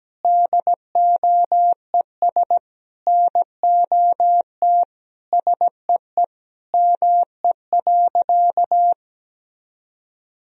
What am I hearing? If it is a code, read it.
Morse code, heard as a keyed tone, about 17 wpm, 700 Hz. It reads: DOES NOT SEE ME.
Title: Sentences from Top 1000 Words - Review Part 3 - 17wpm